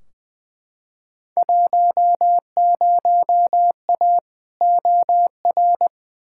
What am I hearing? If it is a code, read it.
Morse code, heard as a keyed tone, about 20 wpm, 700 Hz. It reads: 10A OR
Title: Field Day Exchanges - 20wpm